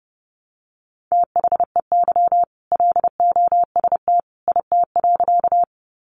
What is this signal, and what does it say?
Morse code, heard as a keyed tone, about 30 wpm, 700 Hz. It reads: THEY LOST IT.
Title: Sentences from Top 1400 Words 30wpm